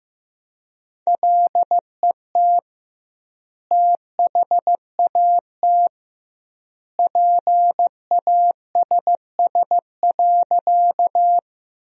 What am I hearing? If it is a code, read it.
Morse code, heard as a keyed tone, about 15 wpm, 700 Hz. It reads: LET THAT PASS.
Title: 2x Word-Spacing - Sentences from Top 400 Words - 15wpm